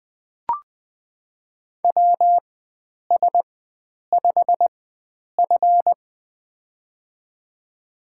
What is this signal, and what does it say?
Morse code, heard as a keyed tone, about 20 wpm, spaced as for 9 wpm, 700 Hz. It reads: WS5F
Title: USA Call Signs - 4 Characters - 4x Character Spacing - 20wpm